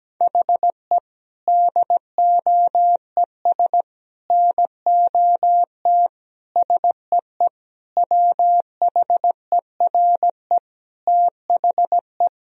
Morse code, heard as a keyed tone, about 17 wpm, 700 Hz. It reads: HE DOES NOT SEE WHERE THE
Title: Sentences from Top 1000 Words - Review Part 1 - 17wpm